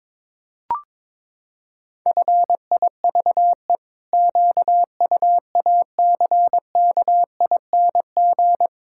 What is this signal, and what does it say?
Morse code, heard as a keyed tone, about 22 wpm, 700 Hz. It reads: FIVE QUACKING
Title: Pangram Challenge v2 - 22wpm